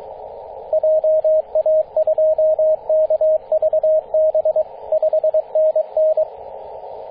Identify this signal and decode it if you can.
Morse code, heard as a keyed tone, about 22 wpm, 630 Hz, with radio noise. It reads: JA2KVB 5NN
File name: 9m0o-24-891mhz-cw